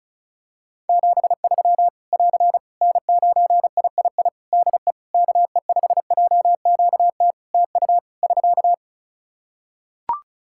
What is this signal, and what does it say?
Morse code, heard as a keyed tone, about 36 wpm, 700 Hz. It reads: 73 <AR> N9III DE KE5JQT TU <SK>
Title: Practice QSO - Exchange 4 - Ending - 35wpm